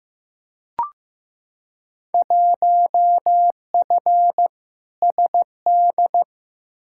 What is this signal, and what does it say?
Morse code, heard as a keyed tone, about 16 wpm, 700 Hz. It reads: 1F SD
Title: Field Day Exchanges - 15wpm